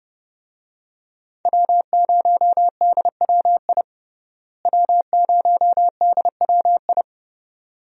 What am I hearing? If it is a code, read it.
Morse code, heard as a keyed tone, about 31 wpm, 700 Hz. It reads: W0DWI W0DWI
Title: USA Call Signs - 5 Characters - Repeated 2x - 30wpm